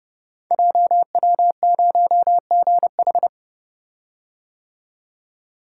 Morse code, heard as a keyed tone, about 28 wpm, 700 Hz. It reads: JW0GH
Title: Code Groups - 5 Letter:Numbers - 30wpm